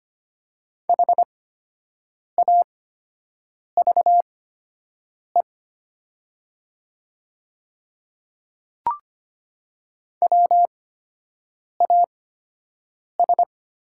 Morse code, heard as a keyed tone, about 25 wpm, spaced as for 7 wpm, 700 Hz. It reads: HAVE WAS
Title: 8x Character Spacing - Top 300 Words - 25wpm